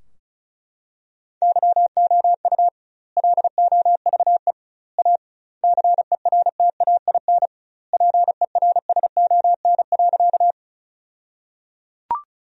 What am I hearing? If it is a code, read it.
Morse code, heard as a keyed tone, about 36 wpm, 700 Hz. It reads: YOU LOVE A CERTAIN PERSON.
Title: 2x Word-Spacing - Sentences from Top 500 Words - 35wpm